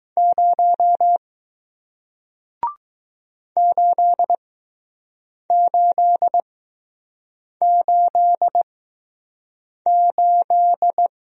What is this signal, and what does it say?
Morse code, heard as a keyed tone, about 21 wpm, spaced as for 6 wpm, 700 Hz. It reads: 0 8888